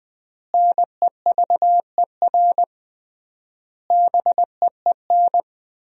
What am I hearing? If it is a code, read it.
Morse code, heard as a keyed tone, about 21 wpm, 700 Hz. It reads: NEVER BEEN
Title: 3x Word-Spacing - Sentences from Top 300 Words - 20wpm